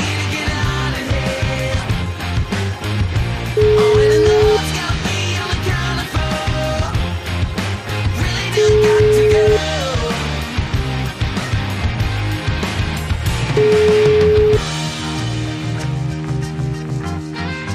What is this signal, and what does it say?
Ringback tone (Europe, 425 Hz), with noise.